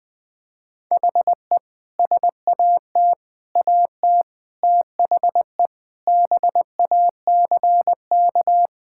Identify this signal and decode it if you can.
Morse code, heard as a keyed tone, about 19 wpm, 700 Hz. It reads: HE SAT AT THE BACK
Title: Sentences with 5-Word Phrases - 20wpm